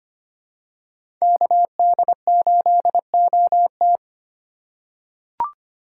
Morse code, heard as a keyed tone, about 25 wpm, 700 Hz. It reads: KD8OT
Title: USA Call Signs - 5 Characters - 3x Character Spacing - 25wpm